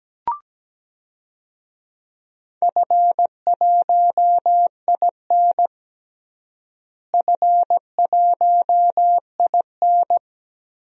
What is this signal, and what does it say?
Morse code, heard as a keyed tone, about 17 wpm, 700 Hz. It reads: F1IN F1IN
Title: DXCC-100 Call Signs - 4 Characters - Repeated 2x - 17wpm